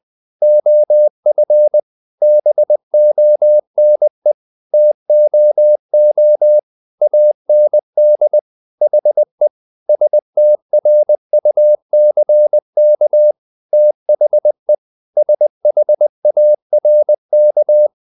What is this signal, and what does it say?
Morse code, heard as a keyed tone, about 20 wpm, 600 Hz. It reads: OF BONE TOO AND HE STRUCK THE SHARK
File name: the old man and the sea part 2 20wpm